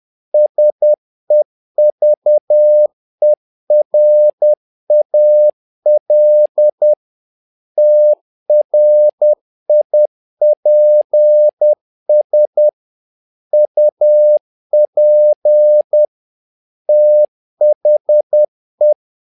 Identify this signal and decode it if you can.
Morse code, heard as a keyed tone, about 10 wpm, 600 Hz. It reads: SEVERAL TRIPS UP THE
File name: learning center / tower trouble / tower trouble 10wpm